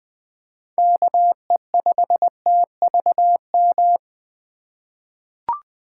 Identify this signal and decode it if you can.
Morse code, heard as a keyed tone, about 20 wpm, 700 Hz. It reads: KE5TVM
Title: USA Call Signs - 4x Character Spacing - 20wpm